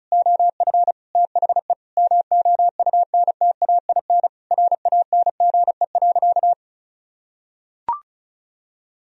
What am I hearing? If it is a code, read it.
Morse code, heard as a keyed tone, about 36 wpm, 700 Hz. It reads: OF THE MOUNTAIN RANGE.
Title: Sentences from Top 1000 Words - Review Part 4 - 35wpm